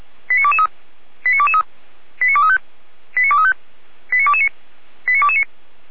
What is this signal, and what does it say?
Five-tone selective call (ZVEI), heard as quick runs of five tones, tones of about 70 ms. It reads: 89223 89223 89235 89235 89229 89229